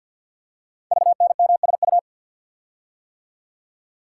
Morse code, heard as a keyed tone, about 48 wpm, 700 Hz. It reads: VNMSU
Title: Code Groups - 5 Letters - 50wpm